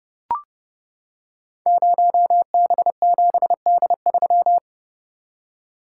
Morse code, heard as a keyed tone, about 30 wpm, 700 Hz. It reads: 0B7D3
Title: Code Groups - 5 Letter:Numbers - 30wpm